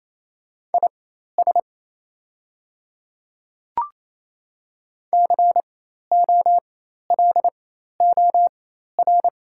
Morse code, heard as a keyed tone, about 27 wpm, spaced as for 13 wpm, 700 Hz. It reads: IS COLOR
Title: 4x Character Spacing - Top 300 Words - 28wpm